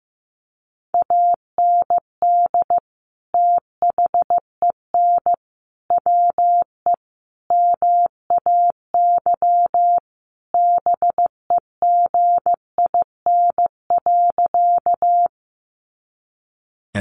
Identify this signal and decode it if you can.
Morse code, heard as a keyed tone, about 15 wpm, 700 Hz. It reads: AND THEN WE MAY BEGIN.
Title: Sentences from Top 300 Words 15wpm